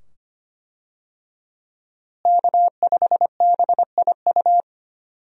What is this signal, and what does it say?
Morse code, heard as a keyed tone, about 25 wpm, 700 Hz. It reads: K5BIU
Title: USA Call Signs - 5 Characters - Repeated 2x - 25wpm